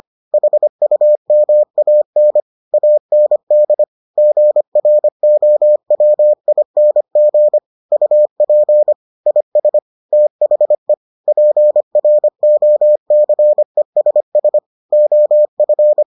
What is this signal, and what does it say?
Morse code, heard as a keyed tone, about 25 wpm, 600 Hz. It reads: HUMAN AND GROWING UP IS THE PROCESS OF